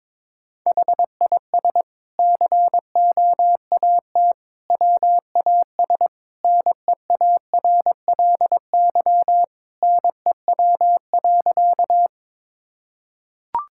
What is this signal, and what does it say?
Morse code, heard as a keyed tone, about 21 wpm, 700 Hz. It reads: HIS COAT WAS NEARLY NEW.
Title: Sentences from Top 1600 Words 22wpm